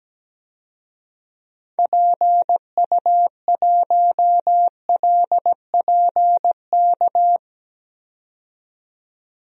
Morse code, heard as a keyed tone, about 17 wpm, 700 Hz. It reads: PU1LPK
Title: DXCC-100 Call Signs - 6 Characters - 17wpm